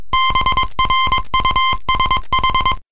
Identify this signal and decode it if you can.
Morse code, heard as a keyed tone, about 22 wpm, 1040 Hz. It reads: BRUSH